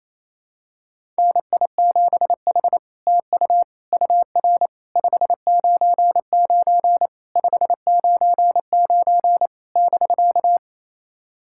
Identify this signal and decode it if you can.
Morse code, heard as a keyed tone, about 28 wpm, 700 Hz. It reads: NI7H TU UR 599 599 <BK>
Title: SOTA QSO Practice 28wpm